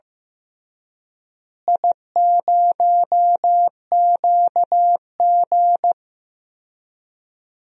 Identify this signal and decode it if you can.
Morse code, heard as a keyed tone, about 15 wpm, 700 Hz. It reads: I0QG
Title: DXCC-100 Call Signs - 4 Characters - Repeated 2x - 15wpm